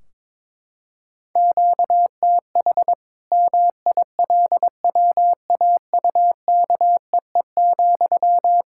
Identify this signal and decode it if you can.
Morse code, heard as a keyed tone, about 22 wpm, 700 Hz. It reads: QTH MILWAUKEE,
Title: K1USN SST QSO Practice 22wpm